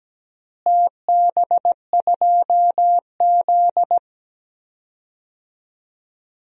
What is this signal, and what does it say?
Morse code, heard as a keyed tone, about 17 wpm, 700 Hz. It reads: TB2Z
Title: DXCC-100 Call Signs - 4 Characters - Repeated 2x - 17wpm